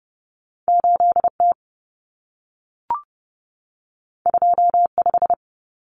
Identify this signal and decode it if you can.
Morse code, heard as a keyed tone, about 30 wpm, 700 Hz. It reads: 8T 25
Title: Common Cut Numbers - 2-Digit 30wpm